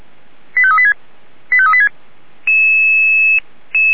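Five-tone selective call (ZVEI), heard as quick runs of five tones, tones of about 70 ms. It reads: 85386 85386
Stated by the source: ZVEI five-tone selective call 89496, received via SDR, from the Feuerwehr Orlamünde fire brigade audio server.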